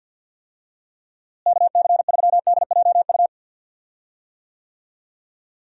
Morse code, heard as a keyed tone, about 49 wpm, 700 Hz. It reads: KC3DJU
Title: USA Call Signs - 6 Characters - Repeated 3x - 50wpm